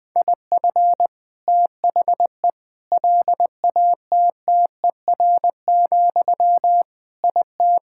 Morse code, heard as a keyed tone, about 20 wpm, 700 Hz. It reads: IF THE LATTER, IT
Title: Sentences from Top 1600 Words 20wpm